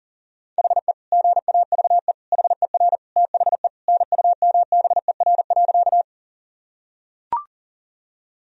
Morse code, heard as a keyed tone, about 40 wpm, 700 Hz. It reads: HE GAVE HER THE NUMBER.
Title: Sentences from Top 1000 Words - Review Part 1 - 40wpm